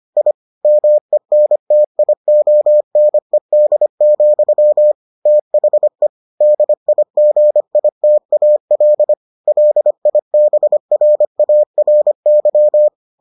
Morse code, heard as a keyed tone, about 25 wpm, 600 Hz. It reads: I MENTIONED, THE DIGITAL LIBRARY